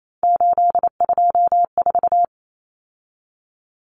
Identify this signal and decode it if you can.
Morse code, heard as a keyed tone, about 28 wpm, 700 Hz. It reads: 824
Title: Numbers - 3-Digit 28wpm